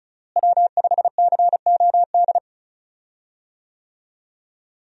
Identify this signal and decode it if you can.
Morse code, heard as a keyed tone, about 36 wpm, 700 Hz. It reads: W5COD
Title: USA Call Signs - 5 Characters - Repeated 3x - 35wpm